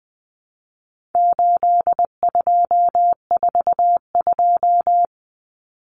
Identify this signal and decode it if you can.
Morse code, heard as a keyed tone, about 20 wpm, 700 Hz. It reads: 8242
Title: Common Cut Numbers - 4-Digit 20wpm